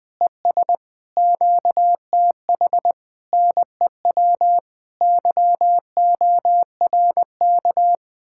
Morse code, heard as a keyed tone, about 20 wpm, 700 Hz. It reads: ES QTH NEW YORK
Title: Speed-Racing Common QSO Phrases - 17wpm